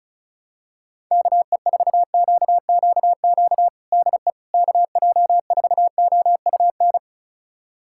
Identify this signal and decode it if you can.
Morse code, heard as a keyed tone, about 36 wpm, 700 Hz. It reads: KE4QQQ DE KJ4OUN
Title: USA Callsign DE Callsign - 6 Characters - 35wpm